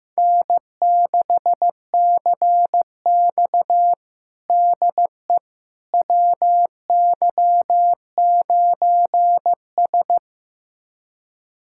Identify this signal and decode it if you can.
Morse code, heard as a keyed tone, about 15 wpm, 700 Hz. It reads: N6CX DE WY9S
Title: USA Callsign DE Callsign - 4 Characters - 15wpm